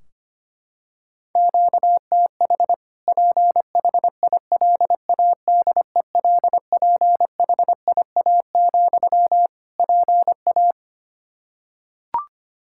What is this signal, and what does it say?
Morse code, heard as a keyed tone, about 25 wpm, 700 Hz. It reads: QTH PHILADELPHIA, PA
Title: Practice QSO - Exchange 1 - Beginning - 25wpm